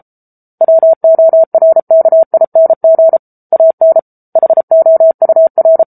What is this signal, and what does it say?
Morse code, heard as a keyed tone, about 34 wpm, 650 Hz. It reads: WORKING AN HOUR
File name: Book Two / 07-Chapter 7 - 33 WPM